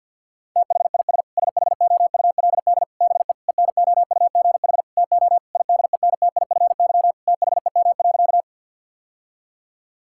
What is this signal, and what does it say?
Morse code, heard as a keyed tone, about 50 wpm, 700 Hz. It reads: THIS SHOULD BE ENOUGH TO IDENTIFY THEM.
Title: Sentences from Top 1000 Words - Review Part 6 - 50wpm